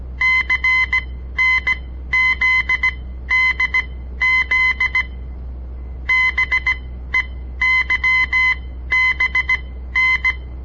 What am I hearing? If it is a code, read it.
Morse code, heard as a keyed tone, about 17 wpm, spaced as for 12 wpm, 2050 Hz. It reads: CNZDZ BEYBN
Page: On the air Morse code practice from the Southern Montana Amatuer Radio Association; all code, no read back